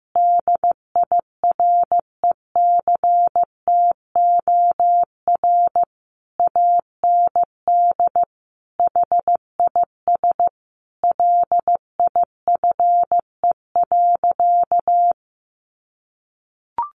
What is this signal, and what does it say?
Morse code, heard as a keyed tone, about 15 wpm, 700 Hz. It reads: DIRECTOR AND HIS LIFE.
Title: Sentences from Top 1300 Words 15wpm